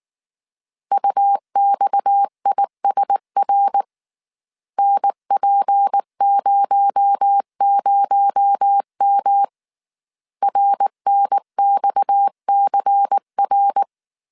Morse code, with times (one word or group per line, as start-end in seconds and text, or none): 0.9-3.8 UXISR
4.8-9.4 NP00M
10.4-13.8 RNXCR